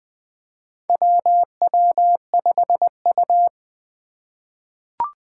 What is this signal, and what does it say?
Morse code, heard as a keyed tone, about 20 wpm, 700 Hz. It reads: WW5U
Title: USA Call Signs - 4 Characters - 4x Character Spacing - 20wpm